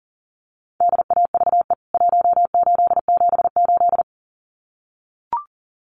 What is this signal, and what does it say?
Morse code, heard as a keyed tone, about 40 wpm, 700 Hz. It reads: DAVE 1878
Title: CWT Exchange 40wpm